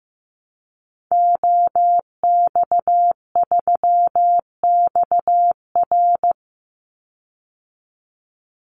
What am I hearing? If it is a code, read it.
Morse code, heard as a keyed tone, about 15 wpm, 700 Hz. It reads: OX3XR
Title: Call Signs Repeated 3x - 15wpm